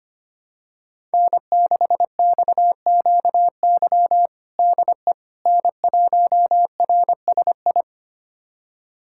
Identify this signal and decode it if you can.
Morse code, heard as a keyed tone, about 25 wpm, 700 Hz. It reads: N6XQY DE N1RSI
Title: USA Callsign DE Callsign - 5 Characters - 25wpm